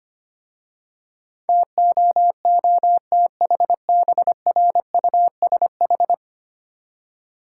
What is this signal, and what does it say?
Morse code, heard as a keyed tone, about 25 wpm, 690 Hz. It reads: TOOTHBRUSH
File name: practice / 25wpm / Easy Words with Hints - 25wpm